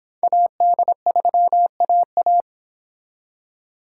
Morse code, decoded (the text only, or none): AD3AA